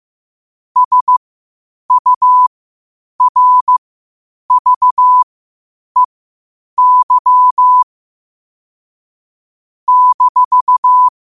Morse code, heard as a keyed tone, about 15 wpm, spaced as for 8 wpm, 1000 Hz. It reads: SURVEY -
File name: qotdfiles / 8wpm